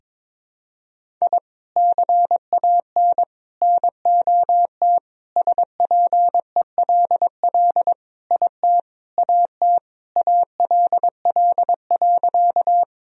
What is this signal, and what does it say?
Morse code, heard as a keyed tone, about 22 wpm, 700 Hz. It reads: I CAN NOT SPELL IT AT ALL.
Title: Sentences from Top 1000 Words - Review Part 1 - 22wpm